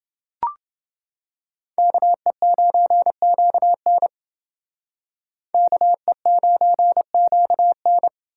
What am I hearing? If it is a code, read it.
Morse code, heard as a keyed tone, about 28 wpm, 700 Hz. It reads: KE9QN KE9QN